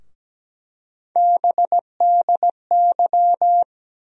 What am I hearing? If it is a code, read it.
Morse code, heard as a keyed tone, about 17 wpm, 700 Hz. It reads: BDY
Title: USA Call Sign Postfixes - 3 Letters 17wpm